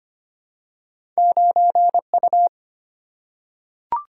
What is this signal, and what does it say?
Morse code, heard as a keyed tone, about 25 wpm, 690 Hz. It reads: 9U
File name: Two Letter-Number Combinations - 25wpm